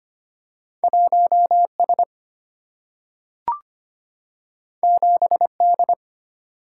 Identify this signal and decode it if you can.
Morse code, heard as a keyed tone, about 25 wpm, 700 Hz. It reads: 1S 7D
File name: Two Letter-Number Combinations - 25wpm